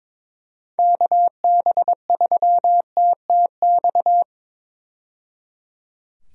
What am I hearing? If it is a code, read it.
Morse code, heard as a keyed tone, about 21 wpm, 700 Hz. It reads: KB3TTX
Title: USA Call Signs - 6 Characters - 22wpm